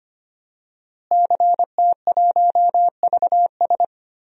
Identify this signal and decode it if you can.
Morse code, heard as a keyed tone, about 25 wpm, 700 Hz. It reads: CT1VS
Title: DXCC-100 Call Signs - 5 Characters - Repeated 2x - 25wpm